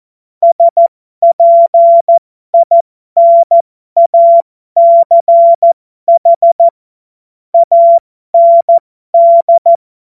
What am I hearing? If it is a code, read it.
Morse code, heard as a keyed tone, about 14 wpm, spaced as for 12 wpm, 680 Hz. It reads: SPINACH AND
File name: Morsekurs / Witze